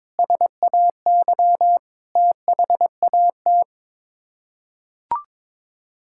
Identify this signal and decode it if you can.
Morse code, heard as a keyed tone, about 22 wpm, 700 Hz. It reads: SAY THAT
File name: practice / 22wpm / Sets of 7 Words - 22wpm